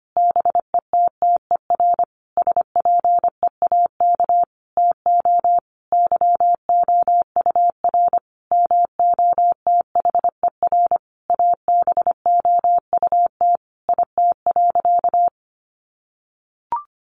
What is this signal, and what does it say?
Morse code, heard as a keyed tone, about 25 wpm, 700 Hz. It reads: BETTER SPEAK TO YOUR MOTHER ABOUT IT.
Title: Sentences from Top 1100 Words 25wpm